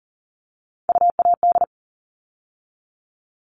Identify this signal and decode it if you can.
Morse code, heard as a keyed tone, about 40 wpm, 700 Hz. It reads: UAD
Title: All Cut Numbers - 3-Digit 40wpm